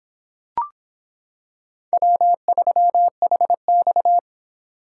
Morse code, heard as a keyed tone, about 26 wpm, 700 Hz. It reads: W3HX